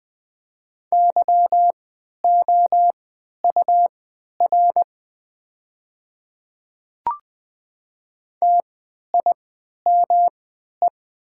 Morse code, heard as a keyed tone, about 20 wpm, spaced as for 11 wpm, 700 Hz. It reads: YOUR TIME
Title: 3x Character Spacing - Top 100 Words - 20wpm